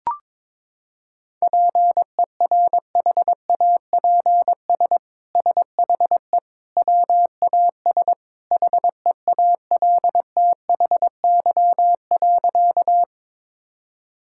Morse code, heard as a keyed tone, about 22 wpm, 700 Hz. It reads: PERHAPS SHE WAS HEALTHY.
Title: Sentences from Top 1000 Words - Review Part 5 - 22wpm